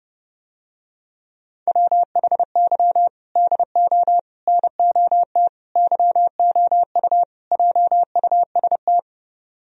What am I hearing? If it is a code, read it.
Morse code, heard as a keyed tone, about 30 wpm, 700 Hz. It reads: WHY DO NOT YOU JUST